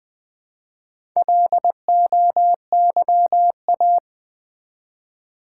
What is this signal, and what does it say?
Morse code, heard as a keyed tone, about 20 wpm, 700 Hz. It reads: LOYA
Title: Build A Word - 1 Letter At A Time - 5-Letter Words - 20wpm